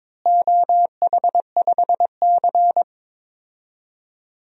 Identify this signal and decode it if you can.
Morse code, heard as a keyed tone, about 22 wpm, 700 Hz. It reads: OH5C
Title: DXCC-100 Call Signs - 4 Characters - Repeated 2x - 22wpm